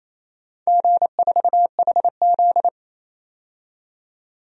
Morse code, heard as a keyed tone, about 28 wpm, 700 Hz. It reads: G4HZ
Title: DXCC-100 Call Signs - 4 Characters - 28wpm